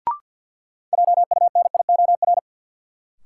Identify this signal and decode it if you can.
Morse code, heard as a keyed tone, about 51 wpm, 700 Hz. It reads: JUNIOR
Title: Sentences from Top 1500 Words - New Word Drill Alt - 50wpm